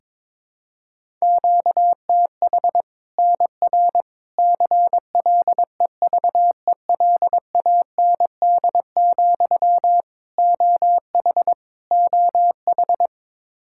Morse code, heard as a keyed tone, about 21 wpm, 700 Hz. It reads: QTH NR CLEVELAND, OH OH
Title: Practice QSO - Exchange 1 - Beginning - 22wpm